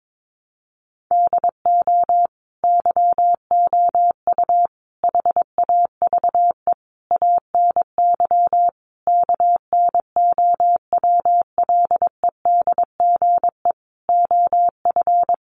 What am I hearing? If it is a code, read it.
Morse code, heard as a keyed tone, about 22 wpm, 700 Hz. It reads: DO YOU HAVE ANY KNOWLEDGE OF
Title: Sentences from Top 600 Words 22wpm